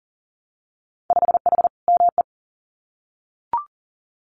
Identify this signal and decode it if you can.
Morse code, heard as a keyed tone, about 40 wpm, 700 Hz. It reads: 5H ME
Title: Winter Field Day Exchanges Encore 40wpm